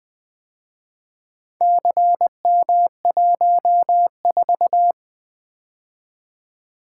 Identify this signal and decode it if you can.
Morse code, heard as a keyed tone, about 25 wpm, 700 Hz. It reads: CM14
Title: Speed-Racing - Four Letter-Number Combinations - 20wpm